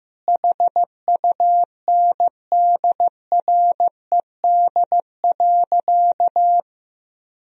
Morse code, heard as a keyed tone, about 15 wpm, 700 Hz. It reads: HUNDRED.
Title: Sentences from Top 1000 Words - Review Part 2 - 15wpm